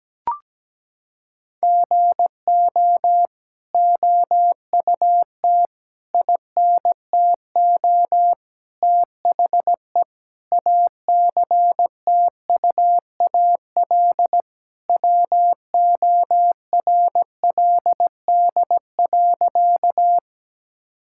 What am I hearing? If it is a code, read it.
Morse code, heard as a keyed tone, about 17 wpm, 700 Hz. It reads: GO OUT INTO THE ACTUAL WORLD.
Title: Sentences from Top 1000 Words - Review Part 5 - 17wpm